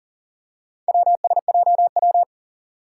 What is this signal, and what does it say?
Morse code, heard as a keyed tone, about 37 wpm, 700 Hz. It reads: WSJW
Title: Four Challenging Letter-Number Combinations 40wpm